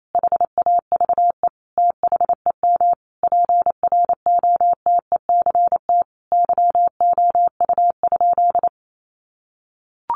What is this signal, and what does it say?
Morse code, heard as a keyed tone, about 28 wpm, 690 Hz. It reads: HAVE THEM PROTECT YOU?
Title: Sentences from Top 900 Words 28wpm